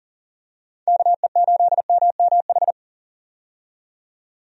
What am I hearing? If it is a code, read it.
Morse code, heard as a keyed tone, about 40 wpm, 700 Hz. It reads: KE8MMH
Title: USA Call Signs - 6 Characters - 40wpm